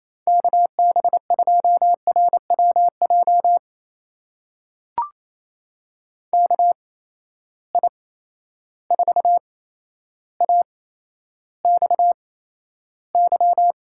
Morse code, heard as a keyed tone, about 28 wpm, spaced as for 12 wpm, 700 Hz. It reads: KB2RWJ KI4AXY
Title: USA Call Signs - 6 Characters - 8x Character Spacing - 28wpm